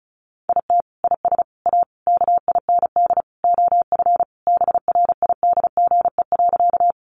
Morse code, heard as a keyed tone, about 34 wpm, 700 Hz. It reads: IT IS A KIND OF BRIDGE.
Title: Sentences from Top 1200 Words 35wpm